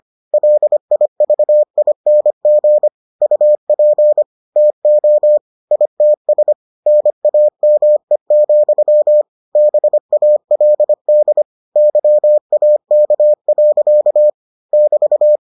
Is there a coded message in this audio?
Yes: Morse code from 0.3 to 15.4 s, heard as a keyed tone, about 25 wpm, 600 Hz. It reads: LIVING UP TO ITS NAME, BALD YAK. <BT>